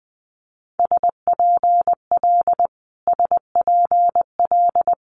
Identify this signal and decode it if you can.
Morse code, heard as a keyed tone, about 20 wpm, 700 Hz. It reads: SPL SPL
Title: Multi-letter Phonemes Repeated 3x 20wpm